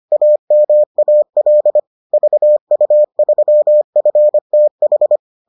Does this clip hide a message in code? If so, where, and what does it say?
Yes: Morse code from 0.1 to 5.2 s, heard as a keyed tone, about 25 wpm, 600 Hz. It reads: AMAL VU3FTH